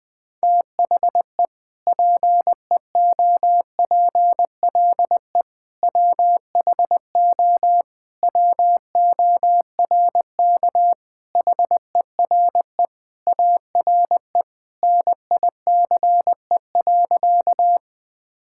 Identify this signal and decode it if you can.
Morse code, heard as a keyed tone, about 20 wpm, 700 Hz. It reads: THE PEOPLE WHO WORK HERE ARE NICE.
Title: Sentences with 3-Word Phrases - 20wpm